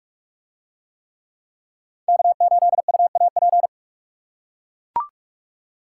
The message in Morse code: K8UAP

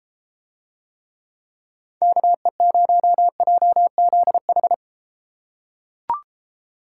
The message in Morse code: KE0JZH

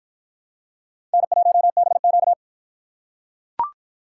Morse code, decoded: N1BQ

